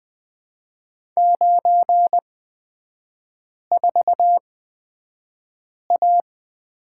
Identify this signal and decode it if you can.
Morse code, heard as a keyed tone, about 20 wpm, spaced as for 5 wpm, 700 Hz. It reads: 94A
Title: 250 Lesson 40e - BK - Three Characters 20wpm